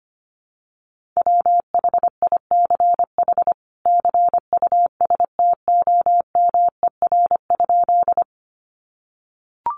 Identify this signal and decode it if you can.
Morse code, heard as a keyed tone, about 25 wpm, 700 Hz. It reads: WHICH CUSTOMER?
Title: Sentences from Top 1100 Words 25wpm